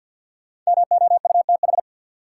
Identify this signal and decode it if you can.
Morse code, heard as a keyed tone, about 50 wpm, 700 Hz. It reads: MOUTH
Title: Sentences from Top 1500 Words - New Word Drill - 50wpm